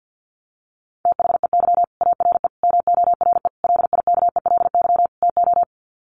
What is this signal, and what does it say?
Morse code, heard as a keyed tone, about 50 wpm, 700 Hz. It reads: THEY ARE MORE LIKELY TO
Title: Sets of 5 Words Encore2 50wpm